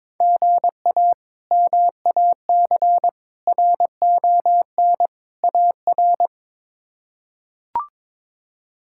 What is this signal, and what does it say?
Morse code, heard as a keyed tone, about 22 wpm, 700 Hz. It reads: GA MAC RON AR
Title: K1USN SST QSO Practice - 22wpm